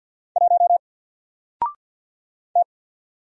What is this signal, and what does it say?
Morse code, heard as a keyed tone, about 50 wpm, 700 Hz. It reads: 1 T
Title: Single Letter-Number 50wpm